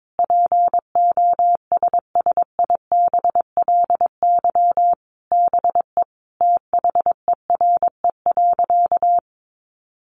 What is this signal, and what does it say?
Morse code, heard as a keyed tone, about 22 wpm, 700 Hz. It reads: POSSIBLY BE THERE.
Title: Sentences from Top 1300 Words 22wpm